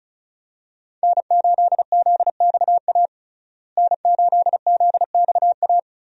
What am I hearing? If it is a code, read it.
Morse code, heard as a keyed tone, about 36 wpm, 700 Hz. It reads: N8ZXA N8ZXA
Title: USA Call Signs - 5 Characters - Repeated 3x - 35wpm